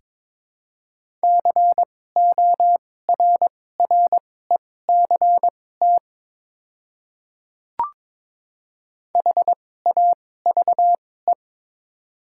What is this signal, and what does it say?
Morse code, heard as a keyed tone, about 22 wpm, spaced as for 16 wpm, 700 Hz. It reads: CORRECT HAVE
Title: 2x Character Spacing - Top 500 Words - 22wpm